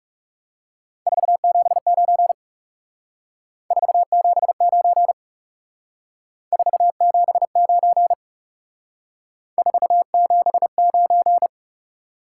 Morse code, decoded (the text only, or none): 479 479 479 479